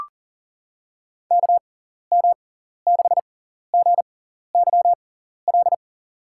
Morse code, heard as a keyed tone, about 40 wpm, spaced as for 14 wpm, 700 Hz. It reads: KM6GYL
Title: USA Call Signs - 6x Character Spacing - 40wpm